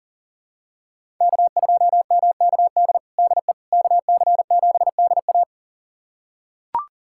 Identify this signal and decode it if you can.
Morse code, heard as a keyed tone, about 40 wpm, 700 Hz. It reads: K2MKD DE KC7DA
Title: USA Callsign DE Callsign - 5 Characters - 40wpm